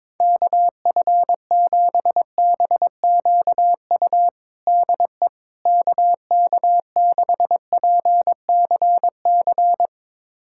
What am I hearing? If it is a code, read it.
Morse code, heard as a keyed tone, about 22 wpm, 700 Hz. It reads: KF7BQU DE KK6PCC
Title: USA Callsign DE Callsign 22wpm